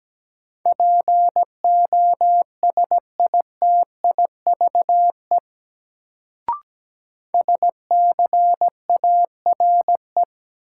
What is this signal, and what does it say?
Morse code, heard as a keyed tone, about 17 wpm, 700 Hz. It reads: POSITIVE SCARE
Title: Sentences from Top 1000 Words - Review Part 5 - New Word Drill - 17wpm